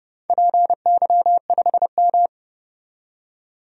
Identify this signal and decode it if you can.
Morse code, heard as a keyed tone, about 30 wpm, 700 Hz. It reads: PY5M
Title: DXCC-100 Call Signs - 4 Characters - 30wpm